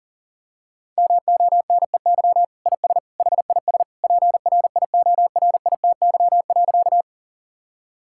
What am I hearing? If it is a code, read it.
Morse code, heard as a keyed tone, about 40 wpm, 700 Hz. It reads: MONEY IS HIS PRIORITY.
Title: Sentences from Top 1600 Words 40wpm